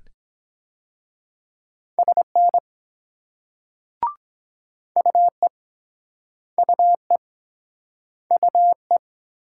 Morse code, heard as a keyed tone, about 22 wpm, 700 Hz. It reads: SN UE UE UE